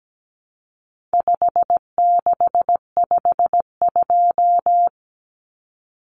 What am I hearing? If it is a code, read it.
Morse code, heard as a keyed tone, about 17 wpm, 700 Hz. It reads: 5652
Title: Common Cut Numbers - 4-Digit 17wpm